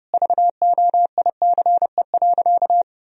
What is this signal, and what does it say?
Morse code, heard as a keyed tone, about 30 wpm, 700 Hz. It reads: VOICE.
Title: Sentences from Top 1000 Words - Review Part 1 - 30wpm